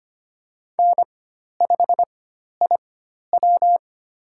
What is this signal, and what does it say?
Morse code, heard as a keyed tone, about 25 wpm, spaced as for 12 wpm, 700 Hz. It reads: N5IW
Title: USA Call Signs - 4 Characters - 4x Character Spacing - 25wpm